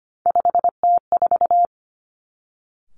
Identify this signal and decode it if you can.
Morse code, heard as a keyed tone, about 25 wpm, 700 Hz. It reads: 5T4
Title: Three Challenging Letter-Number Combinations 25wpm